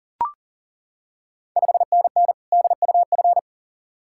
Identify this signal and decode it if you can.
Morse code, heard as a keyed tone, about 40 wpm, 690 Hz. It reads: 5NN DUF
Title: Ontario QSO Party Exchange 40wpm